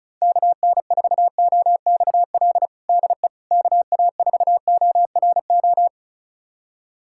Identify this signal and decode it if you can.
Morse code, heard as a keyed tone, about 36 wpm, 700 Hz. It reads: KN4OXL DE KA4ORO
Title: USA Callsign DE Callsign 35wpm